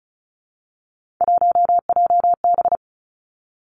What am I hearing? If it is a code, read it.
Morse code, heard as a keyed tone, about 36 wpm, 700 Hz. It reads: 1JB
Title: Three Challenging Letter-Number Combinations 35wpm